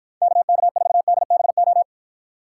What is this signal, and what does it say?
Morse code, heard as a keyed tone, about 53 wpm, 700 Hz. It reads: KK4DBO